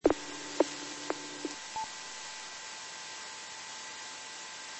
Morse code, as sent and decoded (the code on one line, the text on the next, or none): none